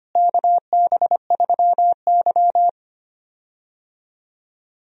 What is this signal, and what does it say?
Morse code, heard as a keyed tone, about 25 wpm, 700 Hz. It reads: KB3Y
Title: USA Call Signs - 4 Characters - Repeated 3x - 25wpm